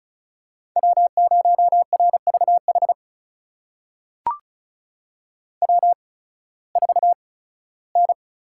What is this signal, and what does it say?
Morse code, heard as a keyed tone, about 36 wpm, spaced as for 19 wpm, 700 Hz. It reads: W0RVH W4N
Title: USA Call Signs - 5 Characters - 8x Character Spacing - 35wpm